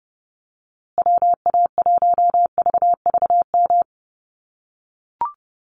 Morse code, heard as a keyed tone, about 30 wpm, 700 Hz. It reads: WA1VVM
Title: USA Call Signs Repeated 2x 30wpm